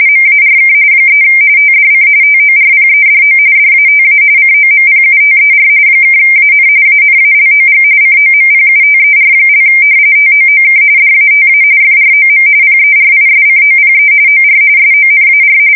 RTTY, about 45.45 baud. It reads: MARK 3 TDC TORPEDO DATA COMPUTER (AN ANALOG COMPUTER) TOOK DATA FROM PERISCOPE OR SONAR
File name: fsk170 / usnavy4